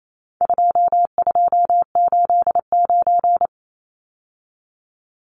Morse code, heard as a keyed tone, about 28 wpm, 700 Hz. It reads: 2289
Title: Numbers - 4-Digit 28wpm